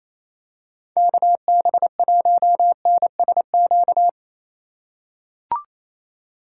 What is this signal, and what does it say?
Morse code, heard as a keyed tone, about 28 wpm, 700 Hz. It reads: KB1NSQ
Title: USA Call Signs - 6 Characters - 4x Character Spacing - 28wpm